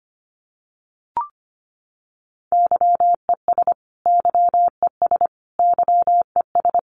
Morse code, heard as a keyed tone, about 25 wpm, 700 Hz. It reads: YES YES YES
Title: Top 500 Words Repeated 25wpm